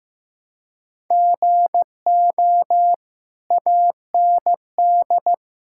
Morse code, heard as a keyed tone, about 15 wpm, 700 Hz. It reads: GO AND
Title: Sentences from Top 1000 Words - Review Part 5 - 15wpm